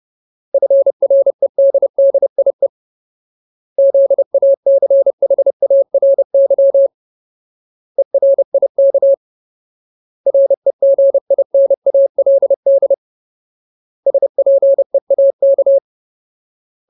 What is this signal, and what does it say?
Morse code, heard as a keyed tone, about 30 wpm, 550 Hz. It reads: FREDDIE ZACHARY ERIK REGINALD SPEAK
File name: Ultimate Word List / 30WPM Ultimate